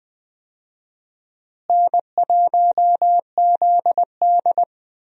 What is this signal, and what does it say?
Morse code, heard as a keyed tone, about 21 wpm, 700 Hz. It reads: N1ZD
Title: USA Call Signs - 4 Characters - Repeated 2x - 20wpm